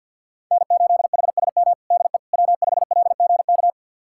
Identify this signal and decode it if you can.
Morse code, heard as a keyed tone, about 53 wpm, 700 Hz. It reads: N8HSM DE W5LGK
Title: USA Callsign DE Callsign - 5 Characters - 50wpm